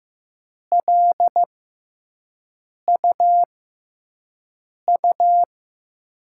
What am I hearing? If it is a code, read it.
Morse code, heard as a keyed tone, about 15 wpm, spaced as for 5 wpm, 700 Hz. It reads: LUU
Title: USA Call Signs - 5 Characters - 6x Character Spacing - 15wpm